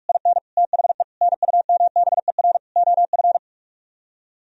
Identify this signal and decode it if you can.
Morse code, heard as a keyed tone, about 49 wpm, 700 Hz. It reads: IN THE NUMBER OF